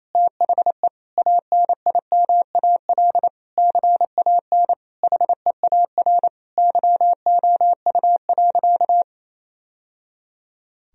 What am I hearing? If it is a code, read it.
Morse code, heard as a keyed tone, about 28 wpm, 700 Hz. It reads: THE ANIMAL CAN HEAR YOU.
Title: Sentences from Top 1000 Words - Review Part 1 - 28wpm